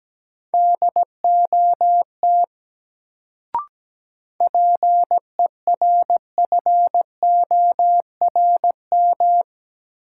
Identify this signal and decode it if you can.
Morse code, heard as a keyed tone, about 17 wpm, 700 Hz. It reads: DOT PERFORM
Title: Sentences from Top 1500 Words - New Word Drill Alt - 17wpm